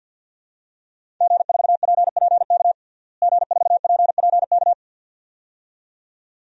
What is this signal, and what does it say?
Morse code, heard as a keyed tone, about 52 wpm, 700 Hz. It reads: G4PPK G4PPK
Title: DXCC-100 Call Signs - 5 Characters - Repeated 2x - 50wpm